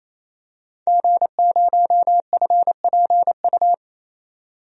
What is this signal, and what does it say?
Morse code, heard as a keyed tone, about 28 wpm, 700 Hz. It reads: G0FPU
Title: DXCC-100 Call Signs - 5 Characters - 28wpm